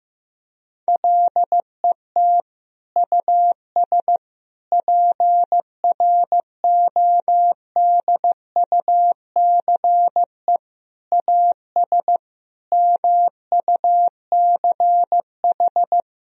Morse code, heard as a keyed tone, about 15 wpm, 700 Hz. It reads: LET US PRODUCE AS MUCH
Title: Sentences from Top 1000 Words - Review Part 3 - 15wpm